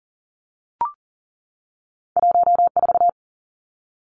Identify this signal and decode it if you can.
Morse code, heard as a keyed tone, about 39 wpm, 700 Hz. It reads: 14
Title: Two Challenging Letter-Number Combinations 40wpm